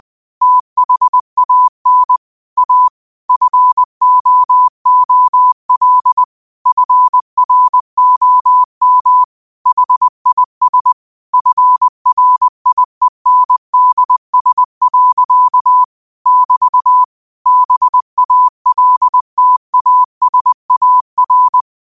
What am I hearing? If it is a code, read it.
Morse code, heard as a keyed tone, about 20 wpm, 1000 Hz. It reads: THAN A FOOL FROM HIS FRIENDS. <BT> BALTASAR
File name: qotdfiles / 20wpm